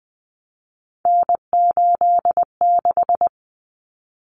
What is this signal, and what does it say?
Morse code, heard as a keyed tone, about 20 wpm, 700 Hz. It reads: N86
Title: Common Cut Numbers - 3-Digit 20wpm